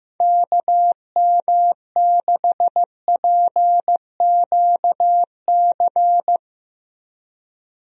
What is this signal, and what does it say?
Morse code, heard as a keyed tone, about 15 wpm, 700 Hz. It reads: KM6PQC
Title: USA Call Signs - 6 Characters - 15wpm